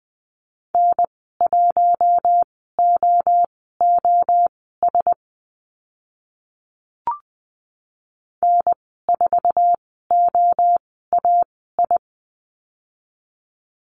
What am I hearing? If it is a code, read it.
Morse code, heard as a keyed tone, about 20 wpm, spaced as for 14 wpm, 700 Hz. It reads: N1OOS N4OAI
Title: 048 Lesson 9g - Number 4 - Call Signs 20wpm